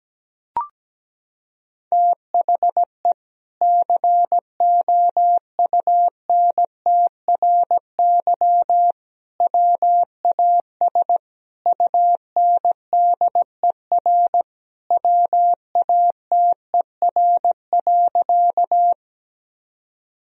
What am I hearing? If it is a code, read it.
Morse code, heard as a keyed tone, about 17 wpm, 700 Hz. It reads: THE COUNTRY WAS UNDER WATER.
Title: Sentences from Top 1000 Words - Review Part 1 - 17wpm